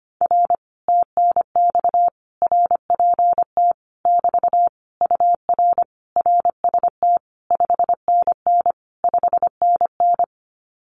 Morse code, heard as a keyed tone, about 25 wpm, 700 Hz. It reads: R TNX RPT <BT> UR RST 5NN 5NN
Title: Common QSO Phrases Encore 25wpm